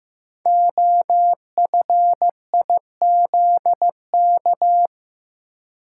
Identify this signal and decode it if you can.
Morse code, heard as a keyed tone, about 15 wpm, 700 Hz. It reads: OFIZK
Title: Code Groups - 5 Letters - 15wpm